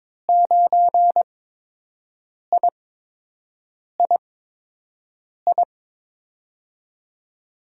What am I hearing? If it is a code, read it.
Morse code, heard as a keyed tone, about 22 wpm, spaced as for 6 wpm, 700 Hz. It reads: 9III
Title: USA Call Signs - 5 Characters - 8x Character Spacing - 22wpm